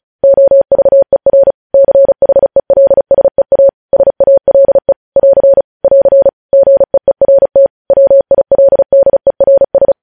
Morse code, heard as a keyed tone, about 36 wpm, 550 Hz. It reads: OVER CHELSEA SALE <AR> <AR> GEERT WILDERS
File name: podcast / News headlines-1x-35x35